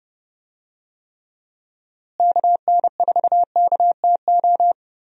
Morse code, heard as a keyed tone, about 29 wpm, 700 Hz. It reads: KN4KTO
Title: USA Call Signs - 6 Characters - Repeated 3x - 30wpm